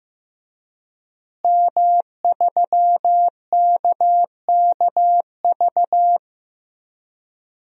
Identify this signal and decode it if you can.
Morse code, heard as a keyed tone, about 15 wpm, 700 Hz. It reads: M3KKV
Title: DXCC-100 Call Signs - 5 Characters - Repeated 3x - 15wpm